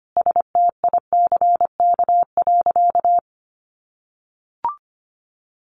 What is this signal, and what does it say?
Morse code, heard as a keyed tone, about 25 wpm, 700 Hz. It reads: STICK.
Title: Sentences from Top 1200 Words 25wpm